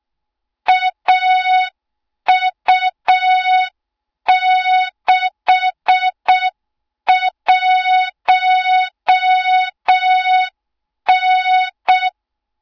Morse code, heard as a keyed tone, about 6 wpm, 3700 Hz. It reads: AU61N
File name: Morze-HA2MN / hang / 20 GYAKORLAT